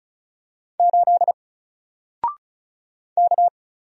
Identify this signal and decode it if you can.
Morse code, heard as a keyed tone, about 36 wpm, 700 Hz. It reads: 8 K
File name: Single-Letter-Number-35wpm